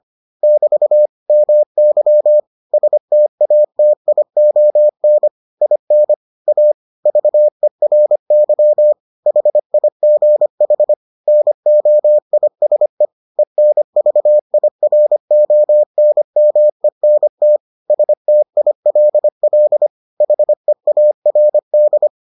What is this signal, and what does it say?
Morse code, heard as a keyed tone, about 25 wpm, 600 Hz. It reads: <BT> MY STATION IN A VERY HIGH NOISE ENVIRONMENT STILL HEARD